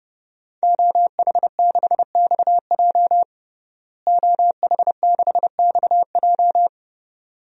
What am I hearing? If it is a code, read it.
Morse code, heard as a keyed tone, about 32 wpm, 700 Hz. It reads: OH6XJ OH6XJ
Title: DXCC-100 Call Signs - 5 Characters - Repeated 2x - 30wpm